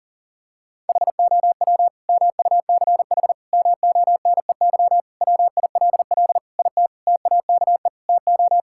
Morse code, heard as a keyed tone, about 40 wpm, 700 Hz. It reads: HOW MUCH MONEY WILL IT TAKE TO
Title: Sentences from Top 1000 Words - Review Part 5 - 40wpm